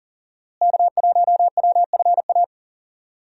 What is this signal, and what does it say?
Morse code, heard as a keyed tone, about 40 wpm, 700 Hz. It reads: K1WFA
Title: USA Call Signs - 5 Characters - Repeated 2x - 40wpm